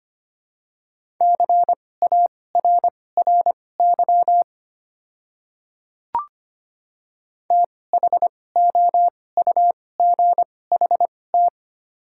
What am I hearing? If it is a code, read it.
Morse code, heard as a keyed tone, about 25 wpm, spaced as for 18 wpm, 700 Hz. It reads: CARRY THOUGHT
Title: 2x Character Spacing - Top 300 Words - 25wpm